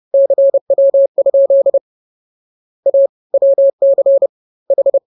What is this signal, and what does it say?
Morse code, heard as a keyed tone, about 30 wpm, spaced as for 24 wpm, 550 Hz. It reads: CW? A WCH